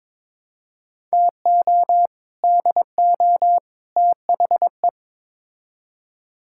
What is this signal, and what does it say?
Morse code, heard as a keyed tone, about 22 wpm, 700 Hz. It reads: TO DO THE
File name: practice / 22wpm / Build A Phrase - 5-Word Phrases - 22wpm